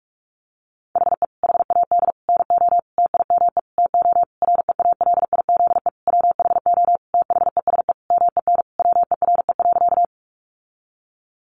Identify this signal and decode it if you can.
Morse code, heard as a keyed tone, about 44 wpm, 700 Hz. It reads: HE HAD NO TIME TO REALIZE WHO THESE MEN WERE.
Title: Sentences from Top 1000 Words 45wpm